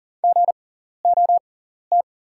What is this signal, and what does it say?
Morse code, heard as a keyed tone, about 40 wpm, spaced as for 14 wpm, 690 Hz. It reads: GOT
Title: 6x Character Spacing - Top 500 Words - 40wpm